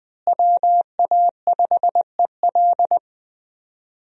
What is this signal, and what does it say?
Morse code, heard as a keyed tone, about 20 wpm, 700 Hz. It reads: WA5EL
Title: USA Call Signs - 5 Characters - Repeated 3x - 20wpm